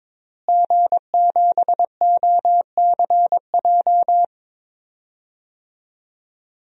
Morse code, heard as a keyed tone, about 21 wpm, 700 Hz. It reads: G7OCJ
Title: DXCC-100 Call Signs - 5 Characters - Repeated 3x - 22wpm